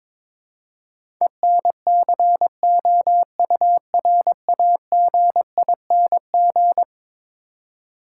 Morse code, heard as a keyed tone, about 22 wpm, 700 Hz. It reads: ENCOURAGING
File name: practice / 22wpm / Eleven Letter Words - 22wpm